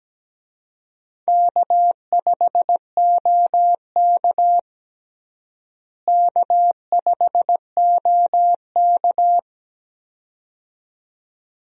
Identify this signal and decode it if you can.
Morse code, heard as a keyed tone, about 17 wpm, 700 Hz. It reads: K5OK K5OK
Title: USA Call Signs - 4 Characters - Repeated 2x - 17wpm